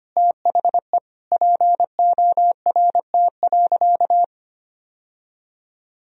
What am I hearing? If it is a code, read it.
Morse code, heard as a keyed tone, about 25 wpm, 700 Hz. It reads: THE PORT.
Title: Sentences from Top 1000 Words - Review Part 1 - 25wpm